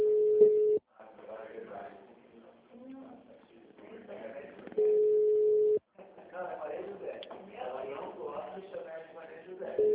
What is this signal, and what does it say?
Ringback tone (Europe, 425 Hz), with speech.